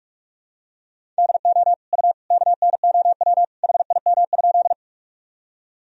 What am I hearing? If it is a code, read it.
Morse code, heard as a keyed tone, about 46 wpm, 700 Hz. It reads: DO U KNOW HIM?
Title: Sentences from Top 100 Words and Abbreviations 45wpm